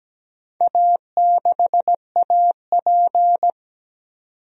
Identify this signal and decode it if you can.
Morse code, heard as a keyed tone, about 17 wpm, 700 Hz. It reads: A6AP
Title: DXCC-100 Call Signs - 4 Characters - 17wpm